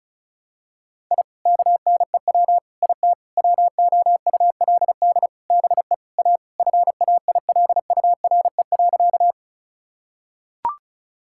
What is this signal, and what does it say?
Morse code, heard as a keyed tone, about 36 wpm, 700 Hz. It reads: I KNEW IT WOULD BE A FAILURE.
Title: Sentences from Top 1000 Words - Review Part 6 - 35wpm